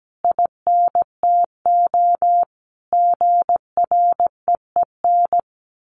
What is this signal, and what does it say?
Morse code, heard as a keyed tone, about 17 wpm, 700 Hz. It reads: INTO GREEN
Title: A Sound Of Thunder 17wpm